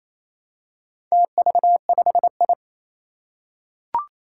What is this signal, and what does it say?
Morse code, heard as a keyed tone, about 27 wpm, 700 Hz. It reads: TV5I
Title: Four Challenging Letter-Number Combinations 28wpm